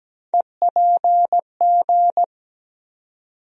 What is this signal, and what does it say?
Morse code, heard as a keyed tone, about 17 wpm, 700 Hz. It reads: EPG
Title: USA Call Sign Postfixes - 3 Letters 17wpm